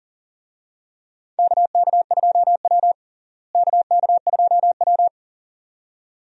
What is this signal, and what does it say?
Morse code, heard as a keyed tone, about 40 wpm, 700 Hz. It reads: KK2W KK2W
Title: USA Call Signs - 4 Characters - Repeated 2x - 40wpm